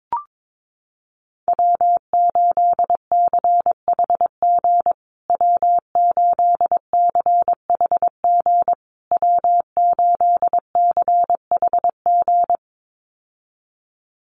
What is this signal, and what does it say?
Morse code, heard as a keyed tone, about 22 wpm, 700 Hz. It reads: W8CHG W8CHG W8CHG
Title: USA Call Signs Repeated 3x 22wpm